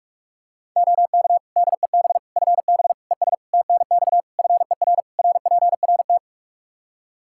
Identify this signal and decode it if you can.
Morse code, heard as a keyed tone, about 46 wpm, 700 Hz. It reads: OK DEB FB ES TNX FER RPRT
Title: Practice QSO - Exchange 2 - Info - 45wpm